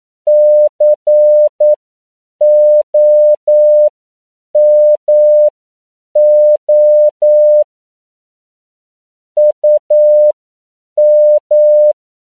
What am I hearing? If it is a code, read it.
Morse code, heard as a keyed tone, about 9 wpm, spaced as for 7 wpm, 600 Hz. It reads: COMO UM
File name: CW / 10PPM